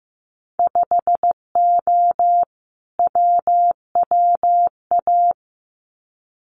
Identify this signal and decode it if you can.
Morse code, heard as a keyed tone, about 15 wpm, 700 Hz. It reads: 5O WWA
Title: Winter Field Day Exchanges Encore 15wpm